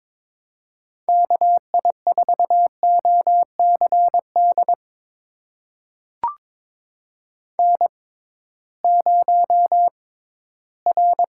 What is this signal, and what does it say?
Morse code, heard as a keyed tone, about 22 wpm, spaced as for 14 wpm, 700 Hz. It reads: KI4OCD N0R
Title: USA Call Signs - 6x Character Spacing - 22wpm